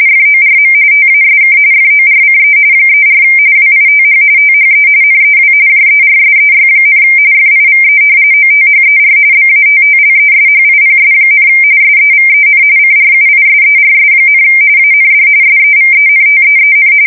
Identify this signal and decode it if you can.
RTTY, about 45.45 baud. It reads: FORM TASK FORCE 34 WITH THE NEW JERSEY, FIVE OTHER BATTLESHIPS, SEVEN CRUISERS, AND SEVENTEEN
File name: fsk170 / battleshipnj